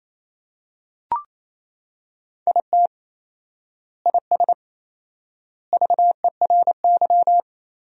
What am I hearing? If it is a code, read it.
Morse code, heard as a keyed tone, about 28 wpm, 700 Hz. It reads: IT IS VERY
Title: 4x Word-Spacing - Sentences from Top 200 Words - 28wpm